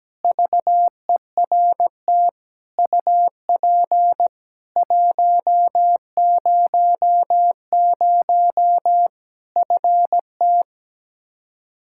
Morse code, heard as a keyed tone, about 17 wpm, 700 Hz. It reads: VERT UP 100 FT
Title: Practice QSO - Exchange 2 - Info - 17wpm